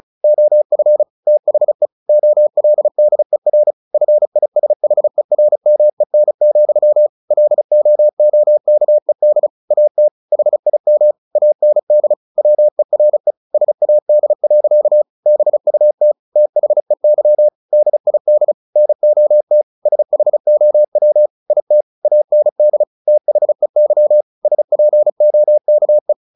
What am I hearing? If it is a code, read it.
Morse code, heard as a keyed tone, about 34 wpm, 600 Hz. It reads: OF THE OLDER FISHERMEN, LOOKED AT HIM AND WERE SAD. BUT THEY DID NOT SHOW IT AND THEY SPOKE
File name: the old man and the sea / the old man and the sea 1 35wpm